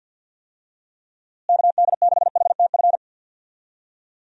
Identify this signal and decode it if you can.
Morse code, heard as a keyed tone, about 51 wpm, 700 Hz. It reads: KD6HTF
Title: USA Call Signs - 6 Characters - Repeated 2x - 50wpm